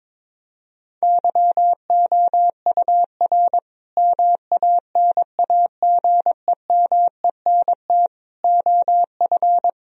Morse code, heard as a keyed tone, about 22 wpm, 700 Hz. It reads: YOUR MANAGEMENT OF
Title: Sentences from Top 1000 Words - Review Part 4 - 22wpm